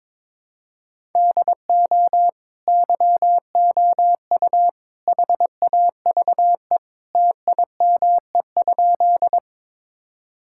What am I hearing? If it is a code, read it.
Morse code, heard as a keyed tone, about 22 wpm, 700 Hz. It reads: DO YOU HAVE TIME?
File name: practice / 22wpm / Sentences with 3-Word Phrases - 22wpm